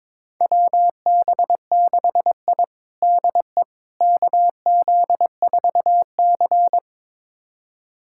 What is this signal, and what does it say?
Morse code, heard as a keyed tone, about 22 wpm, 700 Hz. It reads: WB6I DE KZ4C
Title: USA Callsign DE Callsign - 4 Characters - 22wpm